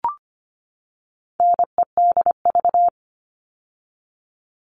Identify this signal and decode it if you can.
Morse code, heard as a keyed tone, about 25 wpm, 700 Hz. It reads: NEDV
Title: All Cut Numbers - 4-Digit 25wpm